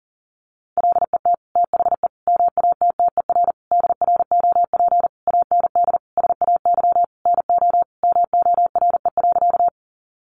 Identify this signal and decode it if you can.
Morse code, heard as a keyed tone, about 40 wpm, 700 Hz. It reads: LET THE MATTER DROP AND SAY NO MORE.
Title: Sentences from Top 1400 Words 40wpm